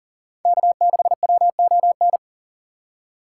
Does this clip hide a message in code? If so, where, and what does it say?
Yes: Morse code from 0.5 to 2.2 s, heard as a keyed tone, about 51 wpm, 700 Hz. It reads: K6WON